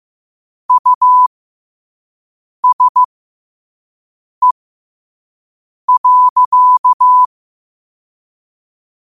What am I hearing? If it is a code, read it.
Morse code, heard as a keyed tone, about 15 wpm, spaced as for 5 wpm, 1000 Hz. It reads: USE.
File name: qotdfiles / 5wpm